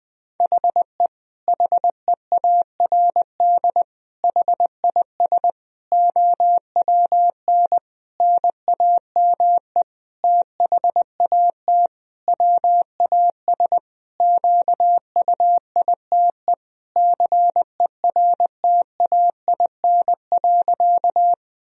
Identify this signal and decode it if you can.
Morse code, heard as a keyed tone, about 20 wpm, 700 Hz. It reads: HE HEARD HIS OWN NAME THAT WAS QUITE CERTAIN.
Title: Sentences from Top 1000 Words - Review Part 2 - 20wpm